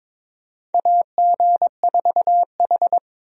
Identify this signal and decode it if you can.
Morse code, heard as a keyed tone, about 22 wpm, 700 Hz. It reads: AG4H
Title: USA Call Signs - 4 Characters - Repeated 2x - 22wpm